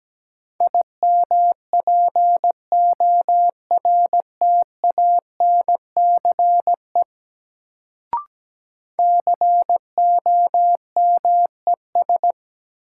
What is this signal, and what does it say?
Morse code, heard as a keyed tone, about 17 wpm, 700 Hz. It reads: IMPORTANCE COMES
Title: Sets of 3 Words - New Word Drill - 17wpm